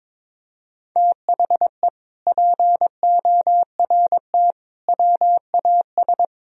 Morse code, heard as a keyed tone, about 22 wpm, 700 Hz. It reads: THE PORT WAS
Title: Sentences from Top 1000 Words - Review Part 1 - 22wpm